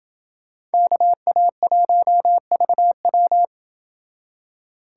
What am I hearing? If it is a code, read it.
Morse code, heard as a keyed tone, about 29 wpm, 700 Hz. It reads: KA1VW